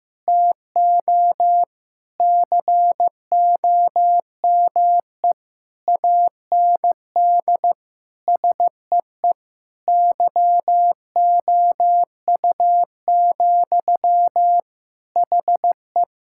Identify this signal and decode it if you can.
Morse code, heard as a keyed tone, about 15 wpm, 700 Hz. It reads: TO COME AND SEE YOU, HE
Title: Sentences from Top 1000 Words - Review Part 6 - 15wpm